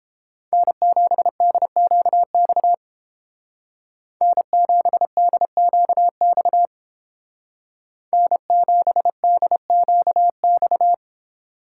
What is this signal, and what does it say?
Morse code, heard as a keyed tone, about 29 wpm, 700 Hz. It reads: N7DQX N7DQX N7DQX